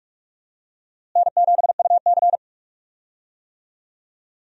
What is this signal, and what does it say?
Morse code, heard as a keyed tone, about 43 wpm, 700 Hz. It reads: N7UC
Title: USA Call Signs - 4 Characters - Repeated 2x - 45wpm